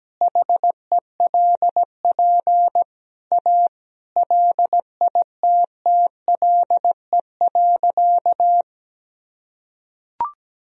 Morse code, heard as a keyed tone, about 17 wpm, 700 Hz. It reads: HELP A LITTLE.
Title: Sentences from Top 1000 Words - Review Part 2 - 17wpm